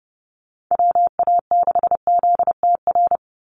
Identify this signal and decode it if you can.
Morse code, heard as a keyed tone, about 28 wpm, 700 Hz. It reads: WA6ZTR
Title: USA Call Signs 30wpm